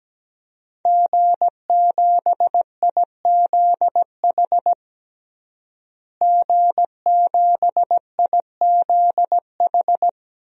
Morse code, heard as a keyed tone, about 17 wpm, 700 Hz. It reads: G7IZH G7IZH
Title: DXCC-100 Call Signs - 5 Characters - Repeated 2x - 17wpm